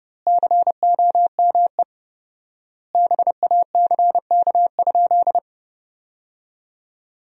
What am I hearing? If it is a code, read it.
Morse code, heard as a keyed tone, about 32 wpm, 700 Hz. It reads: COME BACK?
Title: 4x Word-Spacing - Sentences from Top 200 Words - 30wpm